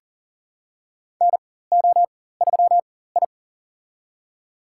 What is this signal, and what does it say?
Morse code, heard as a keyed tone, about 40 wpm, spaced as for 19 wpm, 700 Hz. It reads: NO3I
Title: USA Call Signs - 4 Characters - 4x Character Spacing - 40wpm